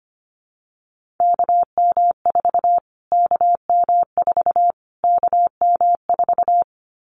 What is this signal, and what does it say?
Morse code, heard as a keyed tone, about 25 wpm, 700 Hz. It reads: KM4 KM4 KM4
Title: Callsign Prefixes for Top 25 DXCC - Repeated 3x - 25wpm